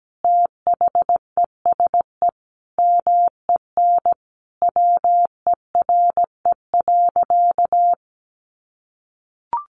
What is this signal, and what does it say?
Morse code, heard as a keyed tone, about 17 wpm, 700 Hz. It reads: THESE MEN WERE.
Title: Sentences from Top 1000 Words 17wpm